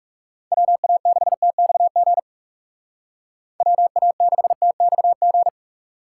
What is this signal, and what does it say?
Morse code, heard as a keyed tone, about 37 wpm, 700 Hz. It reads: WA6TXG WA6TXG
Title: Speed Racing - USA 2x3 Call Signs 30wpm